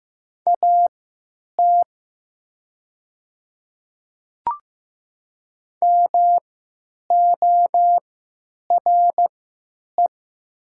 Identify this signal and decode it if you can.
Morse code, heard as a keyed tone, about 15 wpm, spaced as for 9 wpm, 700 Hz. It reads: AT MORE
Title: 3x Character Spacing - Top 100 Words - 15wpm